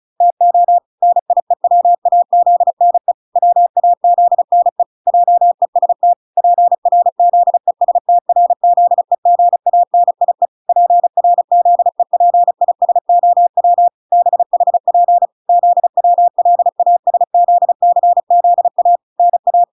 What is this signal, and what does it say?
Morse code, heard as a keyed tone, about 34 wpm, 700 Hz. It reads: TO NIEWAZNE WAZNE JEST PRZESTRZEGANIE PRZEPISOW BHP ZWLASZCZA NA